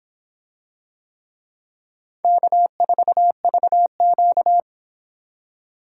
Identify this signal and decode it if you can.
Morse code, heard as a keyed tone, about 24 wpm, 700 Hz. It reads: K4VQ